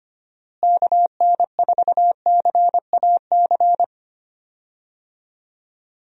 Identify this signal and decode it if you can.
Morse code, heard as a keyed tone, about 25 wpm, 700 Hz. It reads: KN4CAC
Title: USA Call Signs - 6 Characters - Repeated 2x - 25wpm